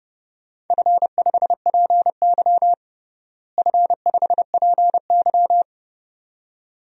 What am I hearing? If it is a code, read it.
Morse code, heard as a keyed tone, about 32 wpm, 700 Hz. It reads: F5PY F5PY
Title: DXCC-100 Call Signs - 4 Characters - Repeated 2x - 30wpm